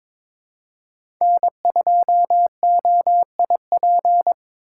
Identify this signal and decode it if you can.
Morse code, heard as a keyed tone, about 22 wpm, 700 Hz. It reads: N2OIP
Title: USA Call Signs - 8x Character Spacing - 22wpm